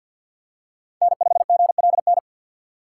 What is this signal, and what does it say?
Morse code, heard as a keyed tone, about 52 wpm, 700 Hz. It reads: N5GLN
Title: USA Call Signs - 5 Characters - 4x Character Spacing - 50wpm